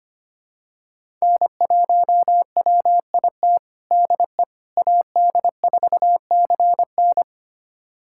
Morse code, heard as a keyed tone, about 25 wpm, 700 Hz. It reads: N1WIT DE AD4CN
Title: USA Callsign DE Callsign - 5 Characters - 25wpm